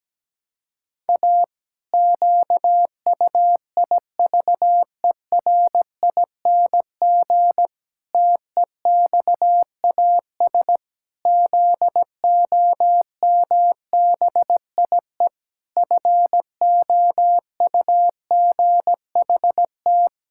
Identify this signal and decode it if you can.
Morse code, heard as a keyed tone, about 17 wpm, 700 Hz. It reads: A QUIVERING TEXAS ZOMBIE FOUGHT
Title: Pangram Challenge v2 - 17wpm